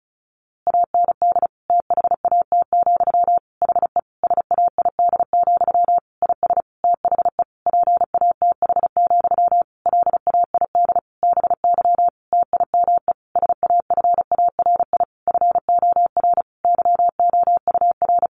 Morse code, heard as a keyed tone, about 34 wpm, 700 Hz. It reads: AND THAT, HE SAID, IS THE PATH, LAID BY TIME SAFARI FOR YOUR
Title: A Sound Of Thunder 35wpm